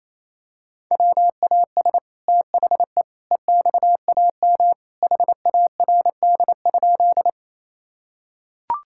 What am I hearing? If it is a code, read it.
Morse code, heard as a keyed tone, about 28 wpm, 700 Hz. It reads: WAS THE EXAM HARD?
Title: Sentences from Top 1000 Words - Review Part 5 - 28wpm